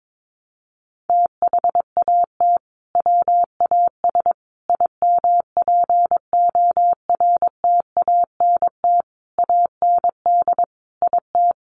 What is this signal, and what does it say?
Morse code, heard as a keyed tone, about 22 wpm, 700 Hz. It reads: THAT WAS IMPORTANT AND IT
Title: Sentences from Top 600 Words 22wpm